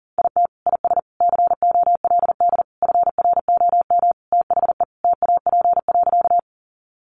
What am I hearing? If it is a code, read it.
Morse code, heard as a keyed tone, about 40 wpm, 700 Hz. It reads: IT IS COLD FROM THE TAP.
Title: Sentences from Top 1400 Words 40wpm